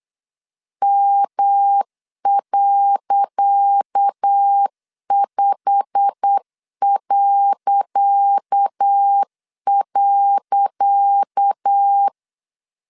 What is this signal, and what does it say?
Morse code, heard as a keyed tone, about 8 wpm, 800 Hz. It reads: M.5..
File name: LEZIONI / 10 SIGNS / 10C DOT 8wpm 3649